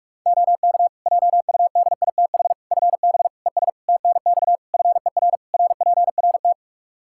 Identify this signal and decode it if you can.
Morse code, heard as a keyed tone, about 46 wpm, 700 Hz. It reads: OK JUDITH FB ES TNX FER RPRT
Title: Practice QSO - Exchange 2 - Info - 45wpm